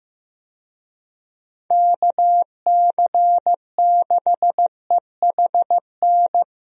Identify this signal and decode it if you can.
Morse code, heard as a keyed tone, about 15 wpm, 700 Hz. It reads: KC6EHN
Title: USA Call Signs - 6 Characters - Repeated 3x - 15wpm